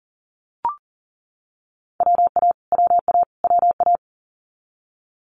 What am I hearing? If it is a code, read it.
Morse code, heard as a keyed tone, about 42 wpm, 690 Hz. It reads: WA WA WA
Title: Multi-letter Phonemes Repeated 3x 40wpm